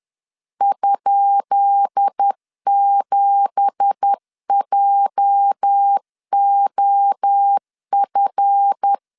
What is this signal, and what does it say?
Morse code, heard as a keyed tone, about 10 wpm, 800 Hz. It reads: ?7JOF
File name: LEZIONI / 10 SIGNS / 10B INT 10wpm 2844